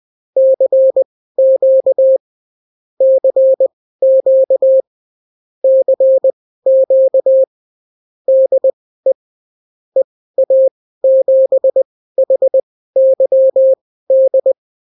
Morse code, heard as a keyed tone, about 20 wpm, spaced as for 14 wpm, 530 Hz. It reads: CQ CQ CQ DE EA7HYD
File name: CW QSO libroCW4 20-10ppm